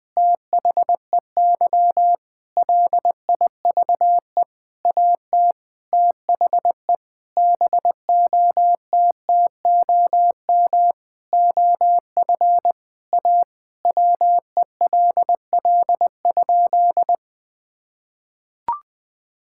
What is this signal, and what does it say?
Morse code, heard as a keyed tone, about 20 wpm, 700 Hz. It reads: THEY LIVE AT THE BOTTOM OF A WELL?
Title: Sentences from Top 1000 Words - Review Part 6 - 20wpm